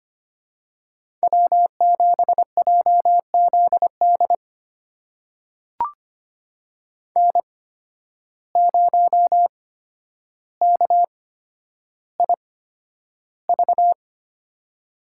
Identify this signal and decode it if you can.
Morse code, heard as a keyed tone, about 25 wpm, spaced as for 11 wpm, 700 Hz. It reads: W7JZD N0KIV
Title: USA Call Signs - 5 Characters - 8x Character Spacing - 25wpm